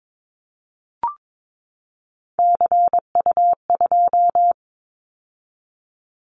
Morse code, heard as a keyed tone, about 22 wpm, 700 Hz. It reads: CU2
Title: Callsign Prefixes for Top 100 DXCC - 22wpm